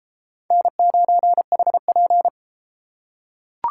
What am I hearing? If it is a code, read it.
Morse code, heard as a keyed tone, about 31 wpm, 700 Hz. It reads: N9HP